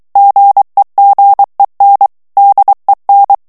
Morse code, heard as a keyed tone, about 23 wpm, 800 Hz. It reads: GEGEN DEN
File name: ALLEMAND texte clair 1 vitesse 22 mots minute